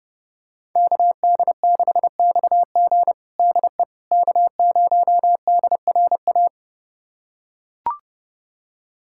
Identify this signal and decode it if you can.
Morse code, heard as a keyed tone, about 30 wpm, 700 Hz. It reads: KD6XG DE K0DRA
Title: USA Callsign DE Callsign - 5 Characters - 30wpm